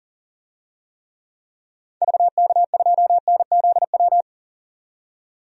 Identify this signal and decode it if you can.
Morse code, heard as a keyed tone, about 39 wpm, 700 Hz. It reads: VK2NZW
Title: DXCC-100 Call Signs - 6 Characters - 40wpm